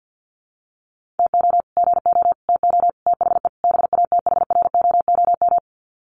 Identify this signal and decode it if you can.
Morse code, heard as a keyed tone, about 48 wpm, 700 Hz. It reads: TO GO TO THE BATHROOM
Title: Sets of 5 Words Encore 50wpm